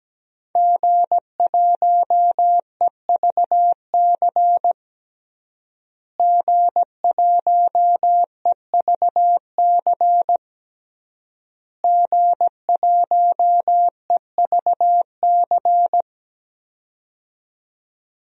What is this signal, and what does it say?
Morse code, heard as a keyed tone, about 17 wpm, 700 Hz. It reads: G1EVC G1EVC G1EVC
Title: DXCC-100 Call Signs - 5 Characters - Repeated 3x - 17wpm